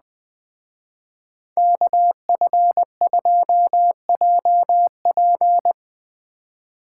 Morse code, heard as a keyed tone, about 20 wpm, 700 Hz. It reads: KF2JP
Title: Speed Racing - USA 2x2 Call Signs 17wpm